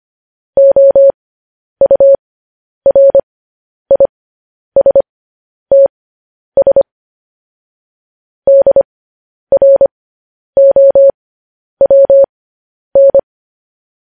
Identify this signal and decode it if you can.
Morse code, heard as a keyed tone, about 25 wpm, spaced as for 10 wpm, 550 Hz. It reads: OURISTS DROWN
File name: podcast / News headlines-1x-25x10